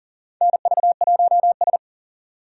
Morse code, heard as a keyed tone, about 40 wpm, 700 Hz. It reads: NV1S
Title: USA Call Signs - 4 Characters - 6x Character Spacing - 40wpm